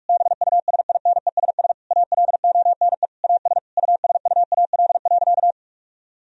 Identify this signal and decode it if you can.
Morse code, heard as a keyed tone, about 46 wpm, 700 Hz. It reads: BUSINESS ALONE AS USUAL.
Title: Sentences from Top 1000 Words - Review Part 2 - 45wpm